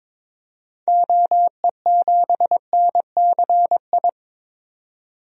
Morse code, heard as a keyed tone, about 22 wpm, 700 Hz. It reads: OE7NCI
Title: DXCC-100 Call Signs - 6 Characters - 22wpm